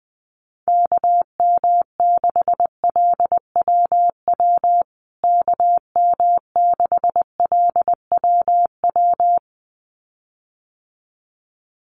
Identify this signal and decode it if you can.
Morse code, heard as a keyed tone, about 20 wpm, 700 Hz. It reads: KM6LWW KM6LWW
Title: USA Call Signs Repeated 2x 20wpm